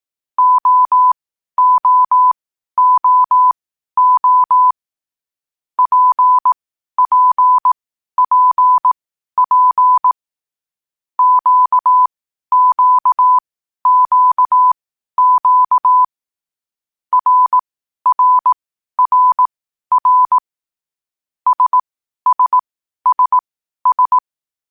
Morse code, heard as a keyed tone, about 18 wpm, spaced as for 12 wpm, 1000 Hz. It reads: OOOO PPPP QQQQ RRRR SSSS
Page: Hier findet Ihr das Morsealphabet in langsamer Geschwindigkeit zum Hören. Dabei ist jeder Buchstabe vier Mal hintereinander aufgenommen worden, danach kommt direkt der nächste.